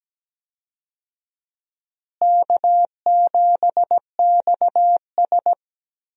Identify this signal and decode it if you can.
Morse code, heard as a keyed tone, about 17 wpm, 700 Hz. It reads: K7XS
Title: USA Call Signs - 4 Characters - Repeated 2x - 17wpm